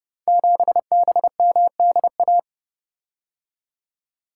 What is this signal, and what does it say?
Morse code, heard as a keyed tone, about 30 wpm, 700 Hz. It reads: 7BMDA
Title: Code Groups - 5 Letter:Numbers - 30wpm